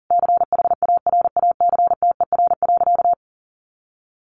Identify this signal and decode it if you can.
Morse code, heard as a keyed tone, about 40 wpm, 700 Hz. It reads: CHARACTER.
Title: Sentences from Top 1200 Words 40wpm